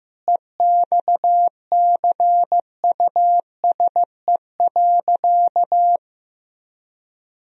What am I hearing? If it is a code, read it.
Morse code, heard as a keyed tone, about 15 wpm, 700 Hz. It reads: EXCUSE.
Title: Sentences from Top 1000 Words - Review Part 4 - 15wpm